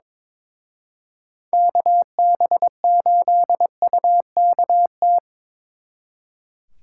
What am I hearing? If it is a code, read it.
Morse code, heard as a keyed tone, about 26 wpm, 700 Hz. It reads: KB8UKT
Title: Speed Racing - USA 2x3 Call Signs 22wpm